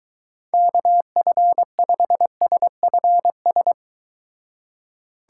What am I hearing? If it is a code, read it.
Morse code, heard as a keyed tone, about 23 wpm, 700 Hz. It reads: KF5SFS